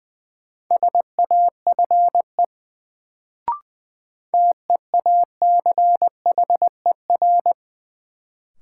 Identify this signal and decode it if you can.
Morse code, heard as a keyed tone, about 20 wpm, 700 Hz. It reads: SAFE TEACHER
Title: Sentences from Top 900 Words - New Word Drill - 20wpm